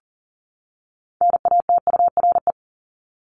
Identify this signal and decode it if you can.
Morse code, heard as a keyed tone, about 40 wpm, 700 Hz. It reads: NATURE
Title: Top 501-750 Words 40wpm